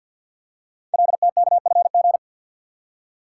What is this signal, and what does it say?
Morse code, heard as a keyed tone, about 48 wpm, 700 Hz. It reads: LTKFG
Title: Code Groups - 5 Letters - 50wpm